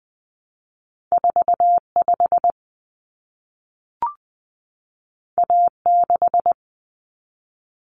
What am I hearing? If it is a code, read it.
Morse code, heard as a keyed tone, about 20 wpm, 700 Hz. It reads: 45 A6
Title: Common Cut Numbers - 2-Digit 20wpm